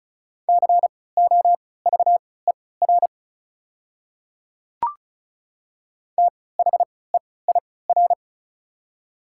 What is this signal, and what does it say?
Morse code, heard as a keyed tone, about 36 wpm, spaced as for 20 wpm, 690 Hz. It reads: COVER THEIR
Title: 3x Character Spacing - Top 300 Words - 35wpm